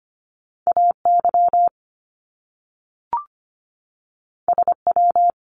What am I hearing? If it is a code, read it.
Morse code, heard as a keyed tone, about 25 wpm, 700 Hz. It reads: AY SW
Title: Top Two Letter Combinations in Words 25wpm